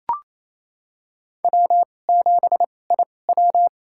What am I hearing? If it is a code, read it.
Morse code, heard as a keyed tone, about 28 wpm, spaced as for 20 wpm, 700 Hz. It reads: W7IW
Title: USA Call Signs - 4 Characters - 2x Character Spacing - 28wpm